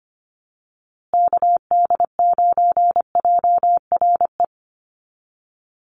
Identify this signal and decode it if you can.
Morse code, heard as a keyed tone, about 25 wpm, 700 Hz. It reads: KD9JRE
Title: USA Call Signs Repeated 2x 25wpm